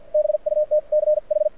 Morse code, heard as a keyed tone, about 50 wpm, spaced as for 42 wpm, 600 Hz. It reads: BUTCH
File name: archived / CRSnet